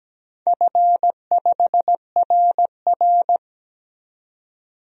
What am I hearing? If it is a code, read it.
Morse code, heard as a keyed tone, about 17 wpm, 700 Hz. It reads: F5RR
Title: DXCC-100 Call Signs - 4 Characters - 17wpm